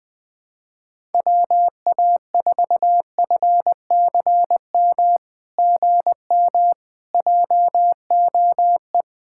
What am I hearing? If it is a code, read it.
Morse code, heard as a keyed tone, about 20 wpm, 700 Hz. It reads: WA4FCM GM JOE
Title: SOTA QSO Practice 20wpm